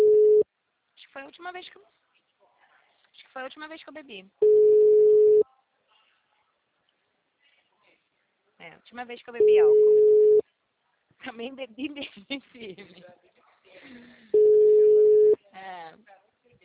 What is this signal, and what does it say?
Ringback tone (Europe, 425 Hz), with speech.